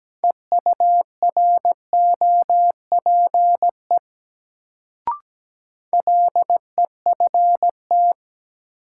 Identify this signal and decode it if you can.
Morse code, heard as a keyed tone, about 17 wpm, 700 Hz. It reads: EUROPE LEFT
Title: Sets of 5 Words Encore - New Word Drill - 17wpm